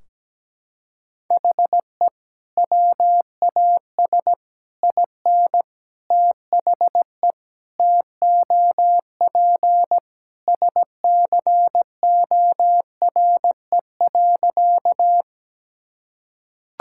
Morse code, heard as a keyed tone, about 17 wpm, 700 Hz. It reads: HE WAS IN THE TOP SCORE.
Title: Sentences from Top 1600 Words 17wpm